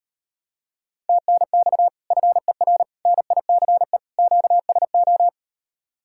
Morse code, heard as a keyed tone, about 38 wpm, 700 Hz. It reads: TNX FER NICE QSO